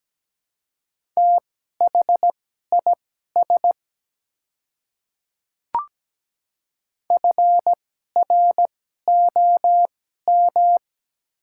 Morse code, heard as a keyed tone, about 17 wpm, spaced as for 12 wpm, 700 Hz. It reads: THIS FROM
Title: 2x Character Spacing - Top 200 Words - 17wpm